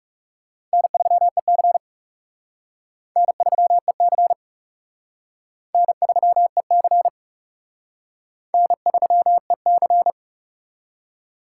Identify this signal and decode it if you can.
Morse code, heard as a keyed tone, about 36 wpm, 700 Hz. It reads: N3EC N3EC N3EC N3EC
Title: Speed Racing - USA 1x2 Call Signs 30wpm